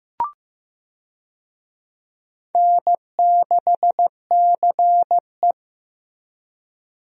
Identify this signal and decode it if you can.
Morse code, heard as a keyed tone, about 15 wpm, 700 Hz. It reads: N6CE
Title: USA Call Signs - 4 Characters - Repeated 3x - 15wpm